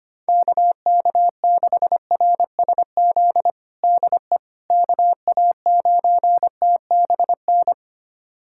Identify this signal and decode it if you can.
Morse code, heard as a keyed tone, about 25 wpm, 700 Hz. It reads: KK6RSZ DE KA9TBN
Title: USA Callsign DE Callsign 25wpm